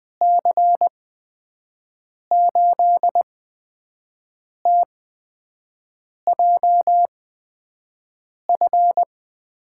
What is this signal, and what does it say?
Morse code, heard as a keyed tone, about 20 wpm, spaced as for 5 wpm, 700 Hz. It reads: C8TJF
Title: USA Call Signs - 6 Characters - 8x Character Spacing - 20wpm